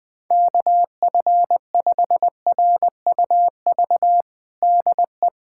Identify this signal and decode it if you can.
Morse code, heard as a keyed tone, about 20 wpm, 700 Hz. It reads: KF5RUV DE
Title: USA Callsign DE Callsign 20wpm